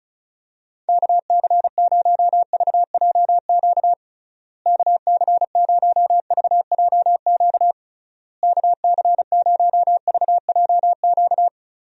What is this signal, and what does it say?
Morse code, heard as a keyed tone, about 36 wpm, 700 Hz. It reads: KC0VJQ KC0VJQ KC0VJQ
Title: USA Call Signs - 6 Characters - Repeated 3x - 35wpm